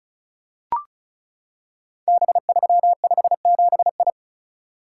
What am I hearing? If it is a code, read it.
Morse code, heard as a keyed tone, about 36 wpm, 690 Hz. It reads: B357I
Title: Code Groups - 5 Letter:Numbers - 35wpm